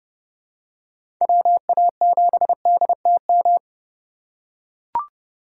Morse code, heard as a keyed tone, about 30 wpm, 700 Hz. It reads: WA7DTM
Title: USA Call Signs - 6 Characters - 2x Character Spacing - 30wpm